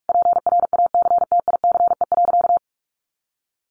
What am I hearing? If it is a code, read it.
Morse code, heard as a keyed tone, about 44 wpm, 700 Hz. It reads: PRACTICE.
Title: Easy Phrases 45wpm